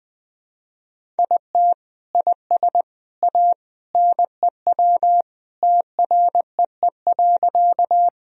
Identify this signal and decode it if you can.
Morse code, heard as a keyed tone, about 21 wpm, 700 Hz. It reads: IT IS A NEW TREE.
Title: Sentences from Top 1000 Words - Review Part 1 - 20wpm